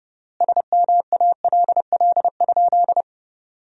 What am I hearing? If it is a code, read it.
Morse code, heard as a keyed tone, about 30 wpm, 700 Hz. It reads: SMALL?
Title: 4x Word-Spacing - Sentences from Top 200 Words - 30wpm